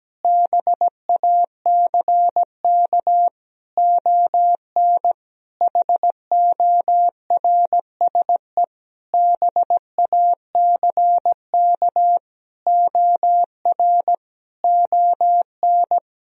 Morse code, heard as a keyed tone, about 17 wpm, 700 Hz. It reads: BACK ON HORSE BACK OR ON
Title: Sentences from Top 1000 Words - Review Part 3 - 17wpm